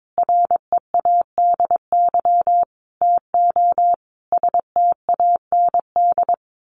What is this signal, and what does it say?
Morse code, heard as a keyed tone, about 22 wpm, 700 Hz. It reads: READY TO STAND
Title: Sentences from Top 1400 Words 22wpm